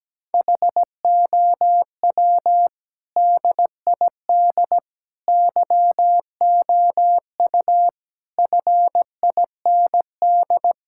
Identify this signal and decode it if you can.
Morse code, heard as a keyed tone, about 17 wpm, 700 Hz. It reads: HOW DID YOU FIND
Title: Sentences from Top 1000 Words - Review Part 5 - 17wpm